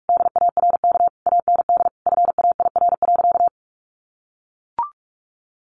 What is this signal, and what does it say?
Morse code, heard as a keyed tone, about 44 wpm, 700 Hz. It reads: DARK AND FAIR.
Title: Sentences from Top 1300 Words 45wpm